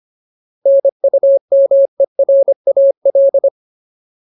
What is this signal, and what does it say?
Morse code, heard as a keyed tone, about 25 wpm, 550 Hz. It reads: NUMERAL
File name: Ultimate Word List / 25WPM Ultimate